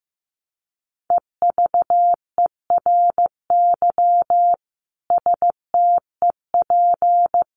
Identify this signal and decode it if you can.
Morse code, heard as a keyed tone, about 15 wpm, 700 Hz. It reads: EVERY STEP
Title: Sets of 5 Words Encore2 15wpm